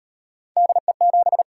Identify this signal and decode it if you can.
Morse code, heard as a keyed tone, about 36 wpm, 700 Hz. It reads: DE7